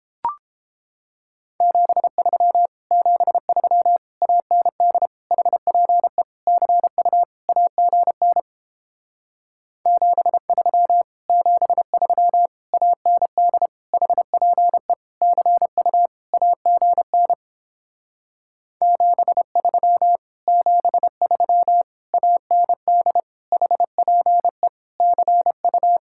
Morse code, heard as a keyed tone, about 29 wpm, 700 Hz. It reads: 73 73 AND HPE CU AGN 73 73 AND HPE CU AGN 73 73 AND HPE CU